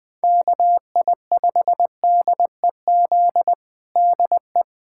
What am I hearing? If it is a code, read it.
Morse code, heard as a keyed tone, about 21 wpm, 700 Hz. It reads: KI5DEZ DE
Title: USA Callsign DE Callsign 20wpm